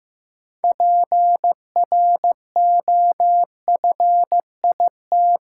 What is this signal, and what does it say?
Morse code, heard as a keyed tone, about 15 wpm, 700 Hz. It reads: PROFIT
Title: Sentences from Top 800 Words - New Word Drill - 15wpm